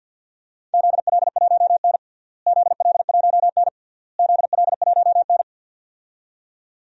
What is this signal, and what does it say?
Morse code, heard as a keyed tone, about 51 wpm, 700 Hz. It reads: ZL1N ZL1N ZL1N
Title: DXCC-100 Call Signs - 4 Characters - Repeated 3x - 50wpm